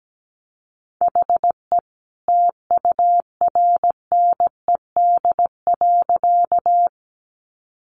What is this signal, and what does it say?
Morse code, heard as a keyed tone, about 17 wpm, 700 Hz. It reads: HE TURNED.
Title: A Sound Of Thunder 17wpm